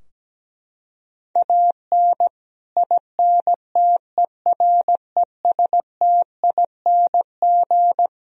Morse code, heard as a keyed tone, about 17 wpm, 700 Hz. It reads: AN INTERESTING
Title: Sentences from Top 1000 Words - Review Part 2 - 17wpm